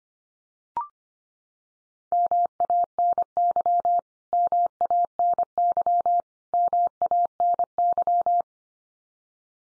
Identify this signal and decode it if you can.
Morse code, heard as a keyed tone, about 25 wpm, 700 Hz. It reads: MANY MANY MANY
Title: Top 100 Words 25wpm x3